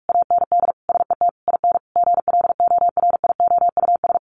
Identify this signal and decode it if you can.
Morse code, heard as a keyed tone, about 46 wpm, 700 Hz. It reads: AND SET IN GLORIOUS
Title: A Sound Of Thunder 45wpm